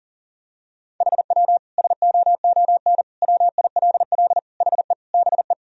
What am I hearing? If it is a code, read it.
Morse code, heard as a keyed tone, about 40 wpm, 700 Hz. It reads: HW SOON WILL HE BE
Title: Sentences from Top 400 Words and Abbreviations 40wpm